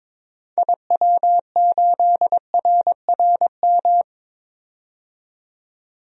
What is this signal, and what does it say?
Morse code, heard as a keyed tone, about 22 wpm, 700 Hz. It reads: IW8RRM
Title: DXCC-100 Call Signs - 6 Characters - 22wpm